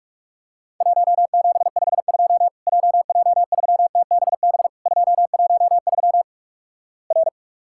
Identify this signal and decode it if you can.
Morse code, heard as a keyed tone, about 48 wpm, 700 Hz. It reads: 1752 JJ3TBB 213 R
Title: Worked All Europe DX Contest - QTC Exchange - 45wpm